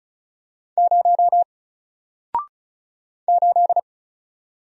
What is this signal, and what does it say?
Morse code, heard as a keyed tone, about 34 wpm, 700 Hz. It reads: 0 8
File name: Single-Letter-Number-35wpm